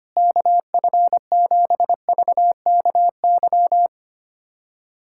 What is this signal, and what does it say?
Morse code, heard as a keyed tone, about 25 wpm, 700 Hz. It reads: KF7VKY
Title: USA Callsign DE Callsign 25wpm